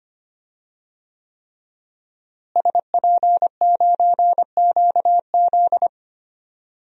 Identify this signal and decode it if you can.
Morse code, heard as a keyed tone, about 25 wpm, 700 Hz. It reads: SP9QZ
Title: DXCC-100 Call Signs - 5 Characters - Repeated 3x - 25wpm